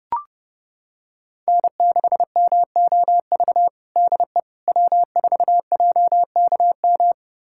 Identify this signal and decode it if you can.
Morse code, heard as a keyed tone, about 30 wpm, 700 Hz. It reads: N6MOV DE W4JKM
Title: USA Callsign DE Callsign - 5 Characters - 30wpm